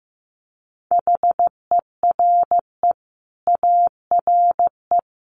Morse code, heard as a keyed tone, about 15 wpm, 700 Hz. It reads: HERE ARE
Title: Sets of 5 Words Encore 15wpm